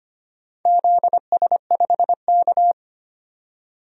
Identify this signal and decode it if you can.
Morse code, heard as a keyed tone, about 25 wpm, 700 Hz. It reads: ZS5K
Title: DXCC-100 Call Signs - 4 Characters - 25wpm